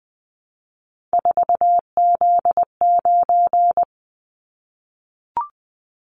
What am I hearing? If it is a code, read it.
Morse code, heard as a keyed tone, about 20 wpm, 700 Hz. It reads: 4Z9
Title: Callsign Prefixes for Top 50 DXCC - 20wpm